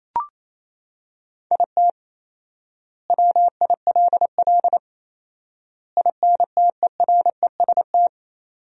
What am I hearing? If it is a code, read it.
Morse code, heard as a keyed tone, about 27 wpm, 700 Hz. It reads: IT WILL INTEREST
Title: 4x Word-Spacing - Sentences from Top 400 Words - 28wpm